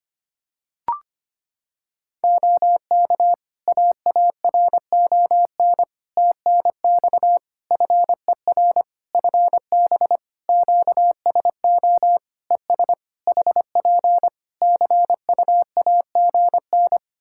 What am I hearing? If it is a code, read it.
Morse code, heard as a keyed tone, about 25 wpm, 700 Hz. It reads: OK AARON TNX FER FB QSO ES HP CUAGN
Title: K1USN SST QSO Practice 25wpm